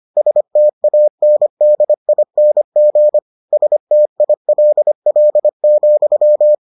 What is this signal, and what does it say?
Morse code, heard as a keyed tone, about 25 wpm, 600 Hz. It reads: STANDING STILL,